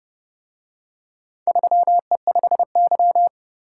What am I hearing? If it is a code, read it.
Morse code, heard as a keyed tone, about 30 wpm, 700 Hz. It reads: 3E5Y